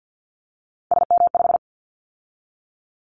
Morse code, heard as a keyed tone, about 50 wpm, 700 Hz. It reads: SM5
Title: Callsign Prefixes for Top 25 DXCC - 50wpm